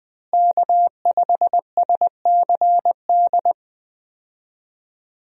Morse code, heard as a keyed tone, about 20 wpm, 700 Hz. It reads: K5SCD
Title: USA Call Signs - 5 Characters - Repeated 2x - 20wpm